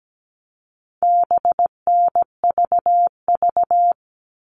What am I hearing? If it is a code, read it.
Morse code, heard as a keyed tone, about 17 wpm, 700 Hz. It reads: BNVV
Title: All Cut Numbers - 4-Digit 17wpm